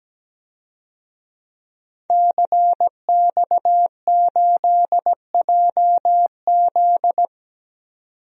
Morse code, heard as a keyed tone, about 17 wpm, 700 Hz. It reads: CX8JZ
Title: DXCC-100 Call Signs - 5 Characters - 17wpm